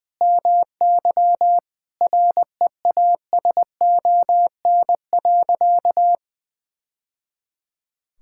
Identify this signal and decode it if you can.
Morse code, heard as a keyed tone, about 20 wpm, 700 Hz. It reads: MY REASON.
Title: Sentences from Top 1000 Words - Review Part 5 - 20wpm